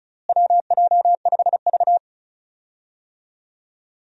Morse code, heard as a keyed tone, about 36 wpm, 700 Hz. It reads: WJ5V
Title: USA Call Signs - 4 Characters - Repeated 2x - 35wpm